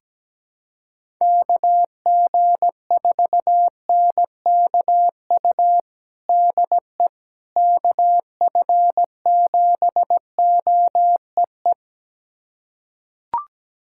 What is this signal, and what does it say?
Morse code, heard as a keyed tone, about 17 wpm, 700 Hz. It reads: KG4NKU DE KF7OEE
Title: USA Callsign DE Callsign - 6 Characters - 17wpm